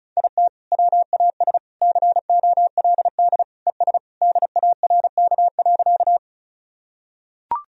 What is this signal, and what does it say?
Morse code, heard as a keyed tone, about 36 wpm, 700 Hz. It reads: IT WAS COLD ES DARK.
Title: Sentences from Top 500 Words and Abbreviations 35wpm